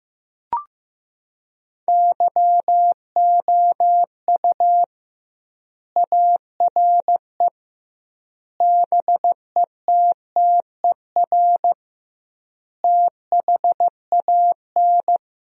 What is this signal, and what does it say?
Morse code, heard as a keyed tone, about 15 wpm, 700 Hz. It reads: YOU ARE BETTER THAN
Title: 2x Word-Spacing - Sentences from Top 400 Words - 15wpm